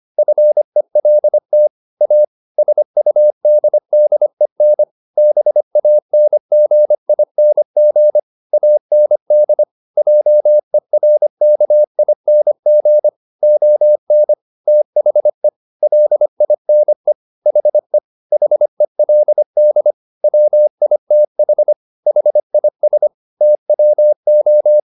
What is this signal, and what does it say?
Morse code, heard as a keyed tone, about 25 wpm, 600 Hz. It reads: FELT A SUDDEN BANGING AND JERKING ON THE LINE HE HELD WITH HIS TWO